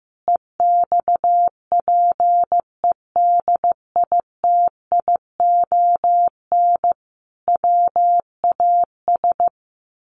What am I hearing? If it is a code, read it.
Morse code, heard as a keyed tone, about 15 wpm, 700 Hz. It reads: EXPEDITION WAS
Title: A Sound Of Thunder 15wpm